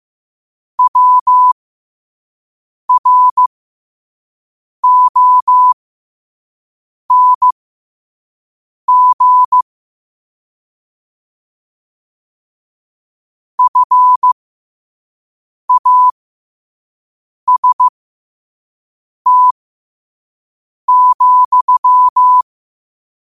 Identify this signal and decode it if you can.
Morse code, heard as a keyed tone, about 15 wpm, spaced as for 5 wpm, 1000 Hz. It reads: WRONG FAST,
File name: qotdfiles / 5wpm